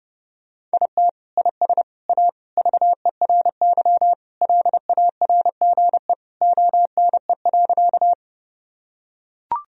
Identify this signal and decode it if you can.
Morse code, heard as a keyed tone, about 30 wpm, 700 Hz. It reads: IT IS A VERY LARGE ONE.
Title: Sentences from Top 1000 Words - Review Part 1 - 30wpm